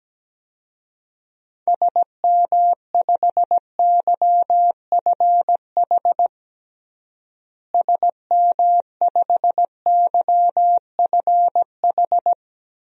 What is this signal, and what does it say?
Morse code, heard as a keyed tone, about 17 wpm, 700 Hz. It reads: SM5YFH SM5YFH
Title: DXCC-100 Call Signs - 6 Characters - Repeated 2x - 17wpm